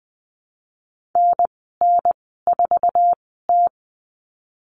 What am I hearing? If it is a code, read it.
Morse code, heard as a keyed tone, about 19 wpm, spaced as for 14 wpm, 700 Hz. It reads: NN4T
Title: 048 Lesson 9g - Number 4 - Call Signs 20wpm